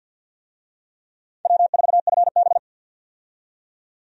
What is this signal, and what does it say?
Morse code, heard as a keyed tone, about 49 wpm, 700 Hz. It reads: W4FB
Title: USA Call Signs - 4 Characters - 50wpm